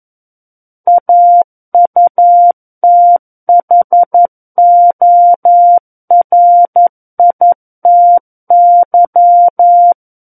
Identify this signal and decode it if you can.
Morse code, heard as a keyed tone, about 11 wpm, 700 Hz. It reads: AUTHORITY
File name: Book One / 02-Chapter 2 - 11 WPM